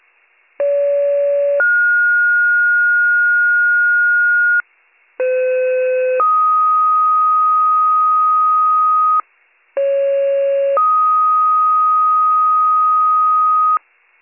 Two-tone page, heard as a one-second tone, then a three-second tone, about 570 then 1430 Hz, about 510 then 1250 Hz, about 550 then 1250 Hz. It is heard from 0.6 to 13.8 s.